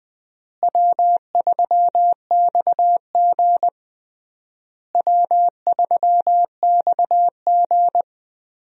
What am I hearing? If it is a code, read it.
Morse code, heard as a keyed tone, about 20 wpm, 700 Hz. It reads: W3XG W3XG
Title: USA Call Signs - 4 Characters - Repeated 3x - 20wpm